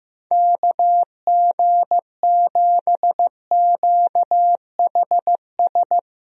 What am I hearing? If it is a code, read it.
Morse code, heard as a keyed tone, about 15 wpm, 700 Hz. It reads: KG7QHS
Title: USA Call Signs - 6 Characters - Repeated 2x - 15wpm